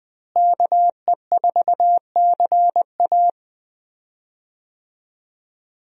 Morse code, heard as a keyed tone, about 20 wpm, 700 Hz. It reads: KE4CA
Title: USA Call Signs - 5 Characters - Repeated 2x - 20wpm